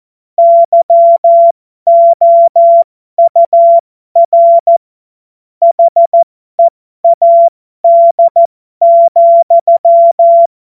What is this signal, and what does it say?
Morse code, heard as a keyed tone, about 14 wpm, spaced as for 12 wpm, 680 Hz. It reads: YOUR HEAD,
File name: Morsekurs / Witze